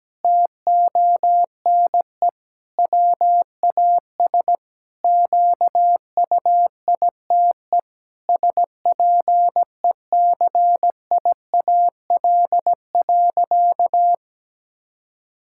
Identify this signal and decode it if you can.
Morse code, heard as a keyed tone, about 17 wpm, 700 Hz. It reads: TONE WAS QUITE SPECIAL.
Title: Sentences from Top 1600 Words 17wpm